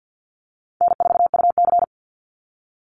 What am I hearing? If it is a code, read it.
Morse code, heard as a keyed tone, about 50 wpm, 700 Hz. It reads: N4UC
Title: Call Signs - 50wpm